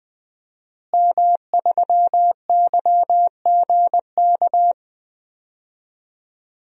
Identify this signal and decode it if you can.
Morse code, heard as a keyed tone, about 20 wpm, 700 Hz. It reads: M3YGK
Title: DXCC-100 Call Signs - 5 Characters - Repeated 2x - 20wpm